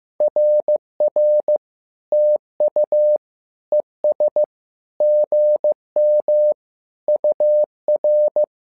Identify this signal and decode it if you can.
Morse code, heard as a keyed tone, about 15 wpm, 600 Hz. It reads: RR TU ES GM UR
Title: SOTA QSO Practice 15wpm